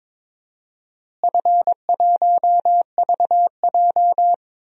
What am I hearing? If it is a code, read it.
Morse code, heard as a keyed tone, about 22 wpm, 700 Hz. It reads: F1VJ
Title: DXCC-100 Call Signs - 4 Characters - Repeated 3x - 22wpm